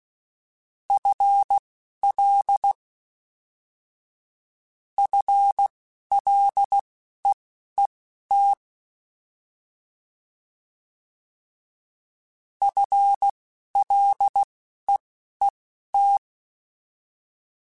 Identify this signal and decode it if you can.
Morse code, heard as a keyed tone, about 16 wpm, spaced as for 11 wpm, 800 Hz. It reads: FL FLEET FLEET
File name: FL / FL mp3s / FL Prefixes 3x2 Twice @10